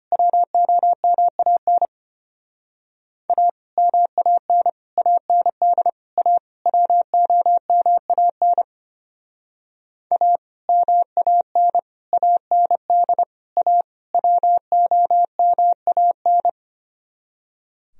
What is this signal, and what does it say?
Morse code, heard as a keyed tone, about 32 wpm, 700 Hz. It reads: WOMAN A MAN AND A WOMAN A MAN AND A WOMAN